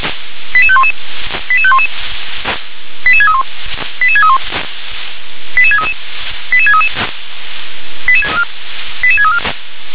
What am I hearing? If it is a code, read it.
Five-tone selective call (ZVEI), heard as quick runs of five tones, tones of about 70 ms. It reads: 88511 88511 88631 88631 88633 88633 88634